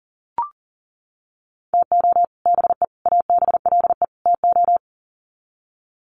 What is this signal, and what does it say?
Morse code, heard as a keyed tone, about 37 wpm, 700 Hz. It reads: TO BE ABLE TO
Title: Sets of 4 Words 40wpm